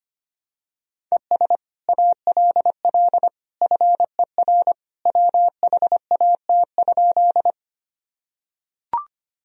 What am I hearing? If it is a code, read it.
Morse code, heard as a keyed tone, about 25 wpm, 700 Hz. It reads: ES ALL FER WHAT?
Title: Sentences from Top 100 Words and Abbreviations 25wpm